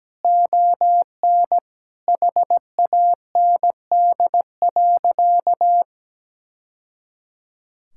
Morse code, heard as a keyed tone, about 17 wpm, 700 Hz. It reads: ON HAND.
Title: Sentences from Top 1000 Words - Review Part 5 - 17wpm